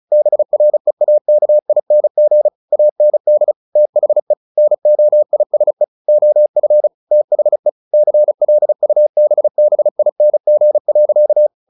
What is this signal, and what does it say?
Morse code, heard as a keyed tone, about 34 wpm, 600 Hz. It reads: BREAKING AND THE NOISE OF THE CLUBBING.
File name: the old man and the sea / the old man and the sea 1 35wpm